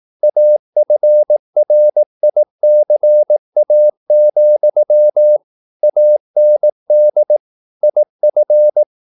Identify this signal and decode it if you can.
Morse code, heard as a keyed tone, about 18 wpm, 600 Hz. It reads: AFRICA, AND IF
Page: Romanen Never af Ken Follet på engelsk.